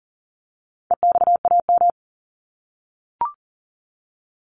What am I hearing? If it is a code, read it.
Morse code, heard as a keyed tone, about 40 wpm, 700 Hz. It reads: EXAM
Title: Top 501-750 Words 40wpm